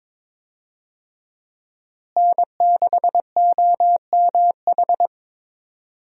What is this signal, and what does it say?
Morse code, heard as a keyed tone, about 22 wpm, 700 Hz. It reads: N6OMH
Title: USA Call Signs - 5 Characters - Repeated 3x - 22wpm